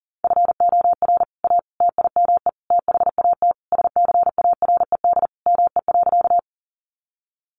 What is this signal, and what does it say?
Morse code, heard as a keyed tone, about 40 wpm, 700 Hz. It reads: FOR A TIME THAT SCARED ME.
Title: Sentences from Top 700 Words 40wpm